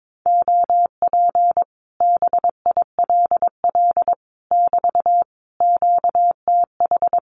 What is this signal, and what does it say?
Morse code, heard as a keyed tone, about 22 wpm, 700 Hz. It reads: OP BILL <BT> QTH
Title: Common QSO Phrases Encore 22wpm